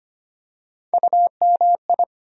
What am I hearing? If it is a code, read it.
Morse code, heard as a keyed tone, about 25 wpm, 700 Hz. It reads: UMI
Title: USA Call Sign Postfixes - 3 Letters 25wpm